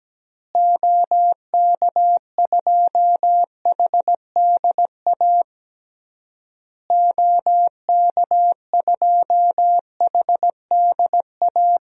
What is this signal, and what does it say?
Morse code, heard as a keyed tone, about 17 wpm, 700 Hz. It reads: OK2HDA OK2HDA
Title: DXCC-100 Call Signs - 6 Characters - Repeated 3x - 17wpm